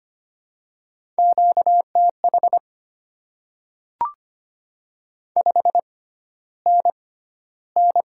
Morse code, heard as a keyed tone, about 25 wpm, spaced as for 13 wpm, 700 Hz. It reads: QTH 5NN
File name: Top 20 QSO Elements - 6x Character Spacing - 25wpm